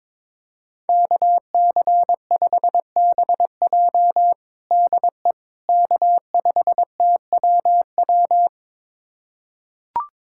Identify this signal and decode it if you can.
Morse code, heard as a keyed tone, about 21 wpm, 700 Hz. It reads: KC5BJ DE K5TWW
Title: USA Callsign DE Callsign - 5 Characters - 22wpm